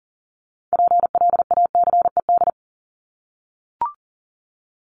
Morse code, heard as a keyed tone, about 40 wpm, 700 Hz. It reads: PLACED
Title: ED Suffix - Top 100 Words with Base Words 40wpm